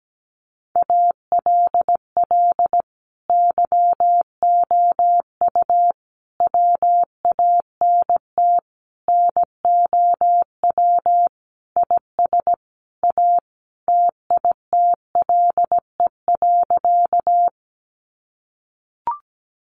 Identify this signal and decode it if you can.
Morse code, heard as a keyed tone, about 17 wpm, 700 Hz. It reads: ALL YOU WANT NOW IS A TITLE.
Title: Sentences from Top 1200 Words 17wpm